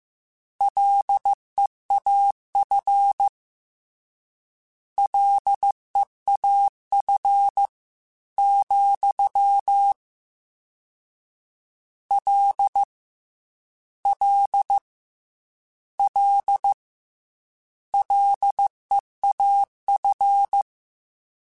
Morse code, heard as a keyed tone, about 15 wpm, 800 Hz. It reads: LEAF LEAF, L L L LEAF
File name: Combinations / Combinations-mp3s / Combinations@13